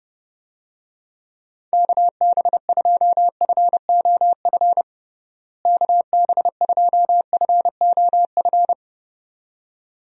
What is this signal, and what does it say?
Morse code, heard as a keyed tone, about 30 wpm, 700 Hz. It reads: KB2FOF KB2FOF
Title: USA Call Signs - 6 Characters - Repeated 2x - 30wpm